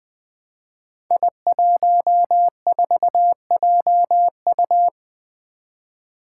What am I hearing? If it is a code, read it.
Morse code, heard as a keyed tone, about 20 wpm, 700 Hz. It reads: I14JU
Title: Five Challenging Letter-Number Combinations 20wpm